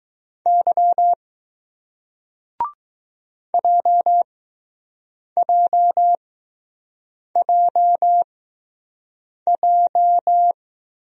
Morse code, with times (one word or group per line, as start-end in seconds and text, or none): 0.5-1.1 Y
3.5-10.5 JJJJ